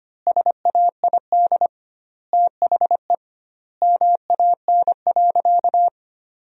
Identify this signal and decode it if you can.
Morse code, heard as a keyed tone, about 25 wpm, 700 Hz. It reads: SAID THE MAN.
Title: 2x Word-Spacing - Sentences from Top 200 Words - 25wpm